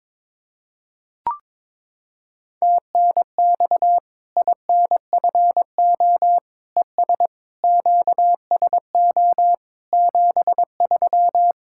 Morse code, heard as a keyed tone, about 22 wpm, 700 Hz. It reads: TNX INFO ES QSO 73
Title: Practice QSO - Exchange 4 - Ending - 22wpm